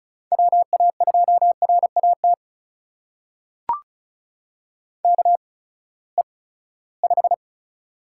Morse code, heard as a keyed tone, about 36 wpm, spaced as for 20 wpm, 700 Hz. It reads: WA2RAT KE5
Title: USA Call Signs - 6 Characters - 8x Character Spacing - 35wpm